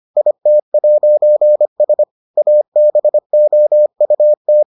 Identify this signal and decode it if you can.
Morse code, heard as a keyed tone, about 25 wpm, 600 Hz. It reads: IT'S ABOUT